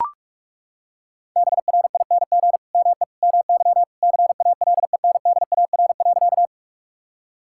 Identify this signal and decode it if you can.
Morse code, heard as a keyed tone, about 42 wpm, 700 Hz. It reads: BRING ME MY CALENDAR.
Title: Sentences from Top 1600 Words 45wpm